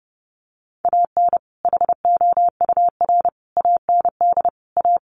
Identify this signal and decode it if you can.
Morse code, heard as a keyed tone, about 30 wpm, 700 Hz. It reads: AN HOUR AND A
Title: Sets of 5 Words 30wpm